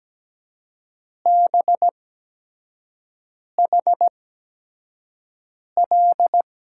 Morse code, heard as a keyed tone, about 17 wpm, spaced as for 5 wpm, 700 Hz. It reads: BHL
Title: USA Call Signs - 6 Characters - 8x Character Spacing - 17wpm